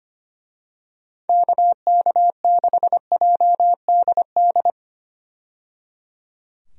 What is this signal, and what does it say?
Morse code, heard as a keyed tone, about 25 wpm, 700 Hz. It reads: KK6JDD
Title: USA Call Signs - 6 Characters - 25wpm